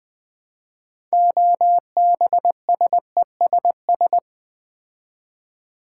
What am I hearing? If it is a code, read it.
Morse code, heard as a keyed tone, about 20 wpm, 700 Hz. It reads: OBSESS
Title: Build A Word - 3 Letters At A Time - 9-Letter Words - 20wpm